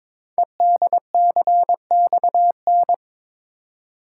Morse code, heard as a keyed tone, about 22 wpm, 700 Hz. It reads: EDCXN
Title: Code Groups - 5 Letters - 22wpm